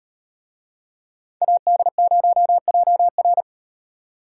Morse code, heard as a keyed tone, about 36 wpm, 700 Hz. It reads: AD0JR